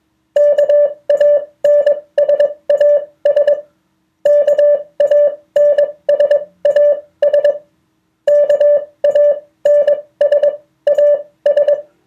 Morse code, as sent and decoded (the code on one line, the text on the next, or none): -.- .- -. ... .- ... / -.- .- -. ... .- ... / -.- .- -. ... .- ...
KANSAS KANSAS KANSAS